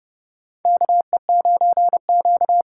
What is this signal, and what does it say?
Morse code, heard as a keyed tone, about 30 wpm, 700 Hz. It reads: KE9Q
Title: USA Call Signs - 4 Characters - 2x Character Spacing - 30wpm